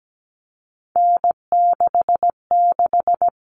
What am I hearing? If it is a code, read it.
Morse code, heard as a keyed tone, about 17 wpm, 700 Hz. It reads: N66
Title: All Cut Numbers - 3-Digit 17wpm